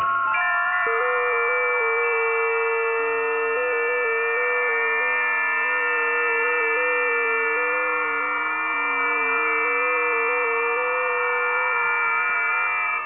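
FT8.
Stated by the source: FT8 test signal received via WebSDR.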